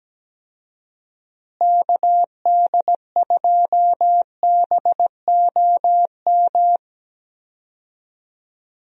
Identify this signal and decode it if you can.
Morse code, heard as a keyed tone, about 17 wpm, 700 Hz. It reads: KD2BOM
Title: USA Call Signs - 6 Characters - Repeated 3x - 17wpm